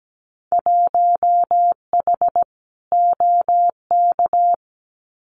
Morse code, heard as a keyed tone, about 17 wpm, 690 Hz. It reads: 1H OK
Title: Winter Field Day Exchanges 17wpm